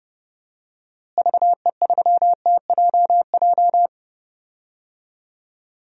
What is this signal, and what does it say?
Morse code, heard as a keyed tone, about 30 wpm, 700 Hz. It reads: VE3TJJ
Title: DXCC-100 Call Signs - 6 Characters - Repeated 3x - 30wpm